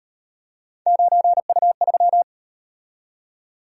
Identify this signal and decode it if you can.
Morse code, heard as a keyed tone, about 37 wpm, 700 Hz. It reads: 9U3